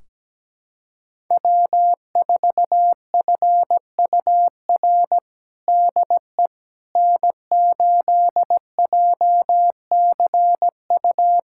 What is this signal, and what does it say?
Morse code, heard as a keyed tone, about 17 wpm, 700 Hz. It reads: W4FUR DE N8JCU
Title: USA Callsign DE Callsign - 5 Characters - 17wpm